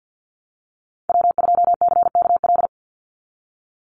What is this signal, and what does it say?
Morse code, heard as a keyed tone, about 50 wpm, 700 Hz. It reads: W2CKL
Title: NA SOTA Activator Callsigns 50wpm